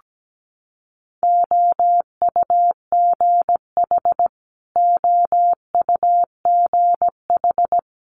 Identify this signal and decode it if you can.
Morse code, heard as a keyed tone, about 17 wpm, 700 Hz. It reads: OUGH OUGH
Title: Multi-letter Phonemes Repeated 3x 17wpm